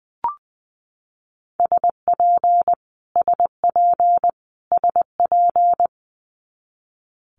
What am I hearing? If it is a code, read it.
Morse code, heard as a keyed tone, about 20 wpm, 700 Hz. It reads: SP SP SP
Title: Multi-letter Phonemes Repeated 3x 20wpm